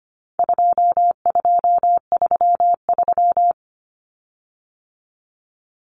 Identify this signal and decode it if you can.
Morse code, heard as a keyed tone, about 25 wpm, 700 Hz. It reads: 2233
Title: Common Cut Numbers - 4-Digit 25wpm